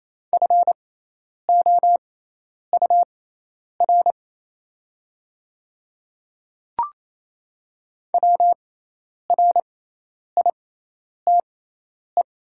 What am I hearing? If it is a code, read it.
Morse code, heard as a keyed tone, about 27 wpm, spaced as for 10 wpm, 700 Hz. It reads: FOUR WRITE
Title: 6x Character Spacing - Top 300 Words - 28wpm